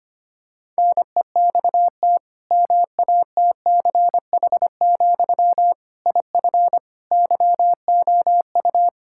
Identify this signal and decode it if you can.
Morse code, heard as a keyed tone, about 25 wpm, 700 Hz. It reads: NEXT MATCH, IF YOU
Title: Sentences from Top 1600 Words 25wpm